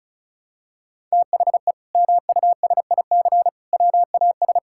Morse code, heard as a keyed tone, about 34 wpm, 700 Hz. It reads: THE MUSIC WAS
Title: Sentences from Top 1000 Words - Review Part 6 - 35wpm